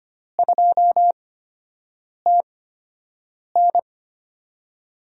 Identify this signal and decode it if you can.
Morse code, heard as a keyed tone, about 25 wpm, spaced as for 7 wpm, 700 Hz. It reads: 2TN
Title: USA Call Signs - 5 Characters - 8x Character Spacing - 25wpm